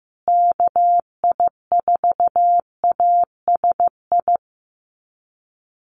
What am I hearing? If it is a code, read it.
Morse code, heard as a keyed tone, about 15 wpm, 700 Hz. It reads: KI4ASI
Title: USA Call Signs Repeated 3x 15wpm